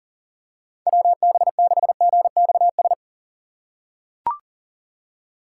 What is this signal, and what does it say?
Morse code, heard as a keyed tone, about 40 wpm, 690 Hz. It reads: WB6GXS
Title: USA Call Signs - 6 Characters - 2x Character Spacing - 40wpm